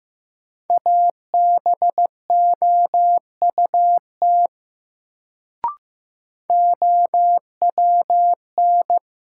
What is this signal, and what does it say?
Morse code, heard as a keyed tone, about 15 wpm, 700 Hz. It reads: ABOUT OWN
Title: Sets of 2 Words - New Word Drill - 15wpm